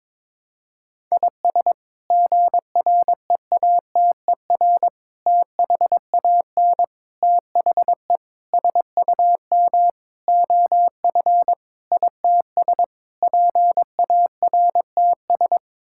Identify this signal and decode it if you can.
Morse code, heard as a keyed tone, about 22 wpm, 700 Hz. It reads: IS GREATER THAN THE SUM OF ITS PARTS
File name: practice / 22wpm / Sets of 8 Words - 22wpm